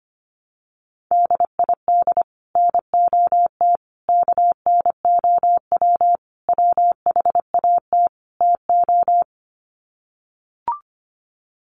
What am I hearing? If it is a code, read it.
Morse code, heard as a keyed tone, about 25 wpm, 700 Hz. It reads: DID NOT KNOW WHAT TO
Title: Sets of 5 Words 25wpm